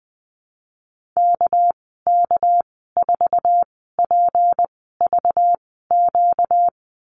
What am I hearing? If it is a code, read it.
Morse code, heard as a keyed tone, about 19 wpm, 700 Hz. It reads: K K 4 P V Q
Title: 238 Lesson 38g - Letter q - Call Signs 20wpm